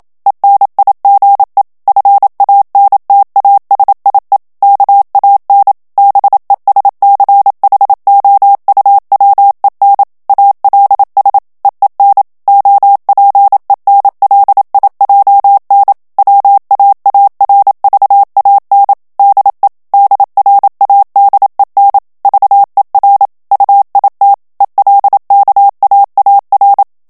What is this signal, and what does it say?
Morse code, heard as a keyed tone, about 27 wpm, 800 Hz. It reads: ENIGE FANTASIE KAN BESCHOUWEN ALS EEN OPENLIJN WAARVAN DE DRADEN VER UIT ELKAAR
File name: HOLLANDAIS texte clair 1 vitesse 26 mots minute